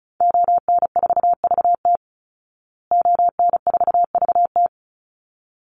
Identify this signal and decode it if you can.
Morse code, heard as a keyed tone, about 34 wpm, 700 Hz. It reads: ON4VT ON4VT
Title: Call Signs Repeated 3x - 35wpm